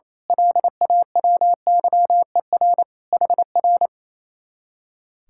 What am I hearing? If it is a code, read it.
Morse code, heard as a keyed tone, about 28 wpm, 700 Hz. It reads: LAWYER HR
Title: Speed-Racing Common QSO Phrases Encore - 28wpm